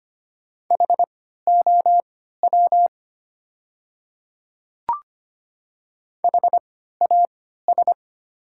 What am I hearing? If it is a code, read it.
Morse code, heard as a keyed tone, about 25 wpm, spaced as for 14 wpm, 700 Hz. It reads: HOW HAS
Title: 3x Character Spacing - Top 100 Words - 25wpm